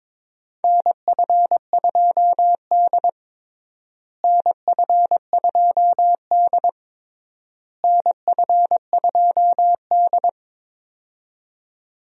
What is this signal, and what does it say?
Morse code, heard as a keyed tone, about 22 wpm, 700 Hz. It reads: NF2D NF2D NF2D
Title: USA Call Signs - 4 Characters - Repeated 3x - 22wpm